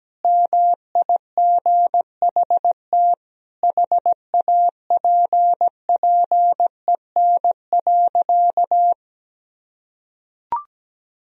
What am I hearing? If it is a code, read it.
Morse code, heard as a keyed tone, about 17 wpm, 700 Hz. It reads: MIGHT HAPPEN.
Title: Sentences from Top 400 Words and Abbreviations 17wpm